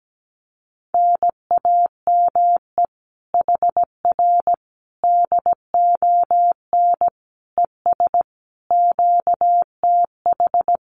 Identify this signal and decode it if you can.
Morse code, heard as a keyed tone, about 17 wpm, 700 Hz. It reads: NAME HR DON ES QTH
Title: Common QSO Phrases Encore 17wpm